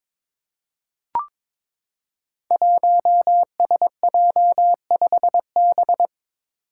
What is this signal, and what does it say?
Morse code, heard as a keyed tone, about 22 wpm, 700 Hz. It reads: 1SJ5B
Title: Five Challenging Letter-Number Combinations 22wpm